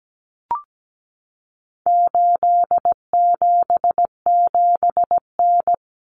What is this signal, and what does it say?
Morse code, heard as a keyed tone, about 17 wpm, 700 Hz. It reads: 877N
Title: Common Cut Numbers - 4-Digit 17wpm